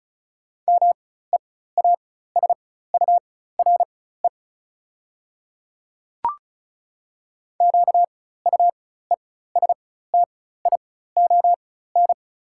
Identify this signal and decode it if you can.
Morse code, heard as a keyed tone, about 35 wpm, spaced as for 16 wpm, 690 Hz. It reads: MEASURE QUESTION
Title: 4x Character Spacing - Top 500 Words - 35wpm